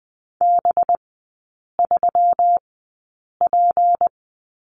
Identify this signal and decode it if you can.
Morse code, heard as a keyed tone, about 19 wpm, spaced as for 8 wpm, 700 Hz. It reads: B3P
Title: 184 Lesson 30e - Letter b - Three Characters 20wpm